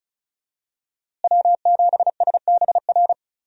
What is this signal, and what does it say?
Morse code, heard as a keyed tone, about 34 wpm, 700 Hz. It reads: W7SBR
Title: USA Call Signs - 5 Characters - Repeated 3x - 35wpm